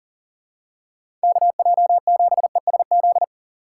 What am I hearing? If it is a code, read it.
Morse code, heard as a keyed tone, about 37 wpm, 700 Hz. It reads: KJ7ESZ
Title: USA Call Signs - 6 Characters - 2x Character Spacing - 40wpm